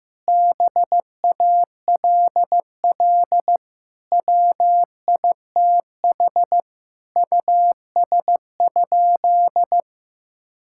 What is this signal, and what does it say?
Morse code, heard as a keyed tone, about 15 wpm, 700 Hz. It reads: BALL WITH US?
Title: Sentences from Top 1000 Words - Review Part 6 - 15wpm